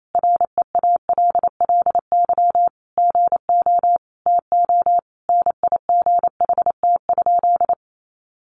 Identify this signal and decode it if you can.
Morse code, heard as a keyed tone, about 28 wpm, 700 Hz. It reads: REALLY GO TO NIGHT?
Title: Sentences from Top 600 Words 28wpm